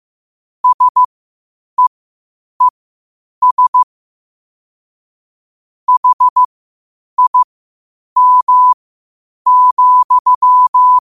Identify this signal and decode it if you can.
Morse code, heard as a keyed tone, about 15 wpm, spaced as for 8 wpm, 1000 Hz. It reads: SEES HIM,
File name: qotdfiles / 8wpm